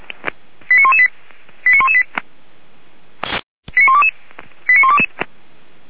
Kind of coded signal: five-tone selective call